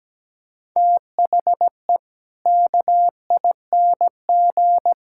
Morse code, heard as a keyed tone, about 17 wpm, 700 Hz. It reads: THE KING
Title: Sentences from Top 1000 Words - Review Part 5 - 17wpm